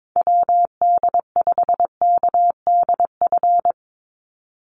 Morse code, heard as a keyed tone, about 22 wpm, 700 Hz. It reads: WD5KDF
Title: USA Call Signs 22wpm